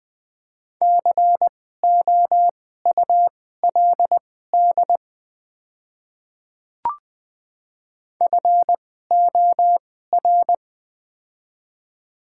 Morse code, heard as a keyed tone, about 20 wpm, spaced as for 15 wpm, 700 Hz. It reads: COULD FOR
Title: 2x Character Spacing - Top 200 Words - 20wpm